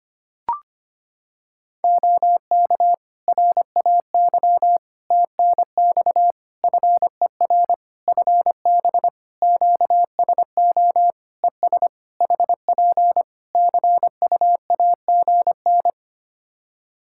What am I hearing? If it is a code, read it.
Morse code, heard as a keyed tone, about 25 wpm, 700 Hz. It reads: OK RAY TNX FER FB QSO ES HP CUAGN
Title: Practice QSO - Exchange 4 - Ending - 25wpm